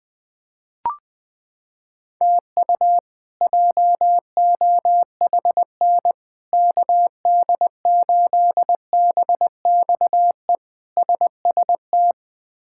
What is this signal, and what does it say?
Morse code, heard as a keyed tone, about 21 wpm, 700 Hz. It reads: TU JOHN KD8BXE SST
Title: K1USN SST QSO Practice - 20wpm